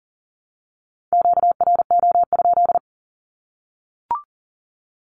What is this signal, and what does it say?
Morse code, heard as a keyed tone, about 40 wpm, 700 Hz. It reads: QRO?
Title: Q-Codes Repeated 3x 40wpm